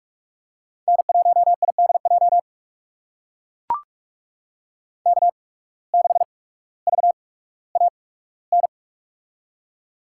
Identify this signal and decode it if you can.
Morse code, heard as a keyed tone, about 43 wpm, spaced as for 19 wpm, 700 Hz. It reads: N1IDJ K6VAN
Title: USA Call Signs - 5 Characters - 8x Character Spacing - 45wpm